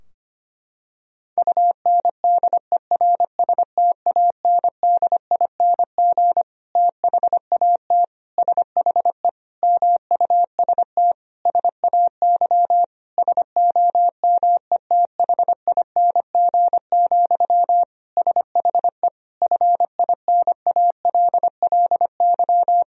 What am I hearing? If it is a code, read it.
Morse code, heard as a keyed tone, about 25 wpm, 700 Hz. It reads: UNDERSTANDING THAT SHE MUST SAY SOMETHING, SHE FINALLY
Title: Sentences from Top 1000 Words - Review Part 3 - 25wpm